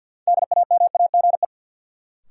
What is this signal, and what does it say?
Morse code, heard as a keyed tone, about 53 wpm, 700 Hz. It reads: DAMAGE
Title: Sentences from Top 1100 Words - New Word Drill Alt - 50wpm